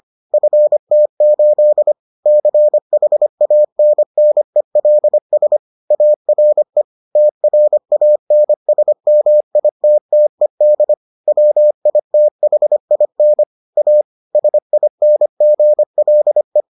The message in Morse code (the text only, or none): FT8 CHANNELS ARE TRANSMITTED WITHIN A SINGLE